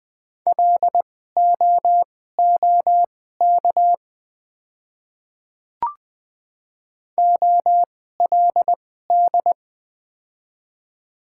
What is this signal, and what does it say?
Morse code, heard as a keyed tone, about 20 wpm, spaced as for 15 wpm, 700 Hz. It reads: LOOK OLD
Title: 2x Character Spacing - Top 200 Words - 20wpm